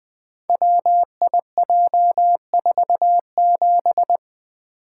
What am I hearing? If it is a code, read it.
Morse code, heard as a keyed tone, about 20 wpm, 700 Hz. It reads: WIJ47
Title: Code Groups - 5 Letter:Numbers - 20wpm